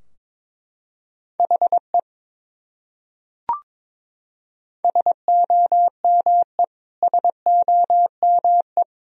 Morse code, heard as a keyed tone, about 22 wpm, 700 Hz. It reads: HE SOME SOME
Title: Top 100 Words Repeated 22wpm